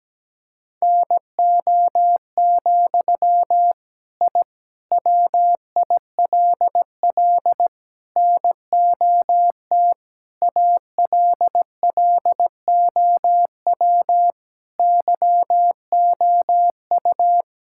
Morse code, heard as a keyed tone, about 17 wpm, 700 Hz. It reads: NO, I WILL NOT ALLOW YOU
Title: Sentences from Top 1000 Words - Review Part 5 - 17wpm